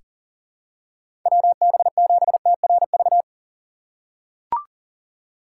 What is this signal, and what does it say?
Morse code, heard as a keyed tone, about 39 wpm, 700 Hz. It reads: WB7TRV
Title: USA Call Signs - 6 Characters - 2x Character Spacing - 40wpm